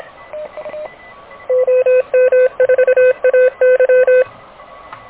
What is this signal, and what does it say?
Morse code, heard as a keyed tone, about 26 wpm, 510 Hz. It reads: OM4AY
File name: truSDX / KVPA ruSDXC